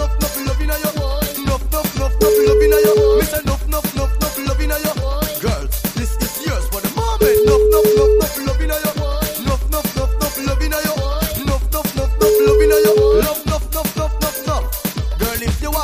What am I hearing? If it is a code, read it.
Ringback tone (Europe, 425 Hz), with noise.